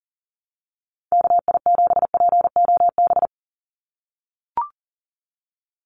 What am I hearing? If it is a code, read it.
Morse code, heard as a keyed tone, about 40 wpm, 700 Hz. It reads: KI7POB
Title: USA Call Signs 40wpm